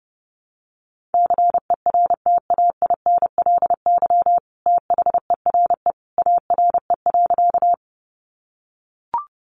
Morse code, heard as a keyed tone, about 30 wpm, 700 Hz. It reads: CERTAINLY THERE ARE.
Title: Sentences from Top 900 Words 30wpm